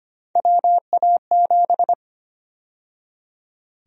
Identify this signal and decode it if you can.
Morse code, heard as a keyed tone, about 25 wpm, 700 Hz. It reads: WA7
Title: USA Call Sign Prefixes - 25wpm